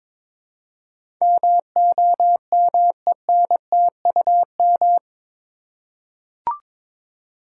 Morse code, heard as a keyed tone, about 22 wpm, 700 Hz. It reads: MOMENTUM
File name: practice / 22wpm / Eight Letter Words - 22wpm